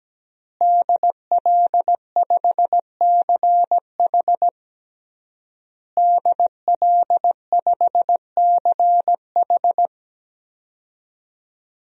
Morse code, heard as a keyed tone, about 17 wpm, 700 Hz. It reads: DL5CH DL5CH
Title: DXCC-100 Call Signs - 5 Characters - Repeated 3x - 17wpm